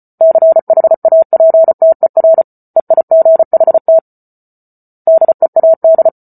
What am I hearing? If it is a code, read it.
Morse code, heard as a keyed tone, about 34 wpm, 650 Hz. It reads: CHAPTER EIGHT DEAD